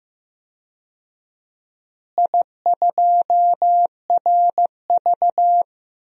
Morse code, heard as a keyed tone, about 15 wpm, 700 Hz. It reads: I2RV
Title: DXCC-100 Call Signs - 4 Characters - Repeated 2x - 15wpm